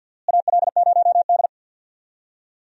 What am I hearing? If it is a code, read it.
Morse code, heard as a keyed tone, about 46 wpm, 700 Hz. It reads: AL0D
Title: USA Call Signs - 4 Characters - Repeated 2x - 50wpm